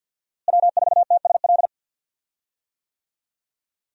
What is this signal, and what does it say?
Morse code, heard as a keyed tone, about 48 wpm, 700 Hz. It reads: W4TSL
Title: USA Call Signs - 5 Characters - Repeated 2x - 50wpm